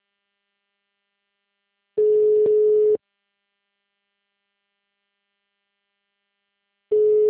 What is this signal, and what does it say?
Ringback tone (Europe, 425 Hz).